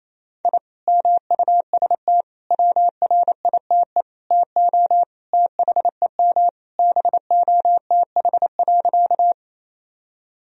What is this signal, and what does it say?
Morse code, heard as a keyed tone, about 28 wpm, 700 Hz. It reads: I MUST WRITE TO THEM BOTH.
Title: Sentences from Top 1000 Words - Review Part 1 - 28wpm